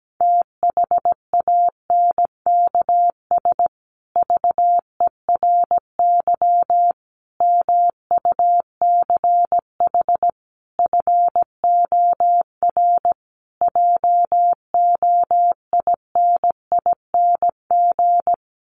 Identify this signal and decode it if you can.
Morse code, heard as a keyed tone, about 17 wpm, 700 Hz. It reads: THANKS VERY MUCH FOR JOINING
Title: Sets of 5 Words Encore2 17wpm